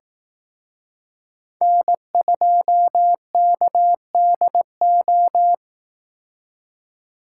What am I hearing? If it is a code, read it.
Morse code, heard as a keyed tone, about 18 wpm, 700 Hz. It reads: N2KDO
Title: Speed Racing - USA 1x3 Call Signs 15wpm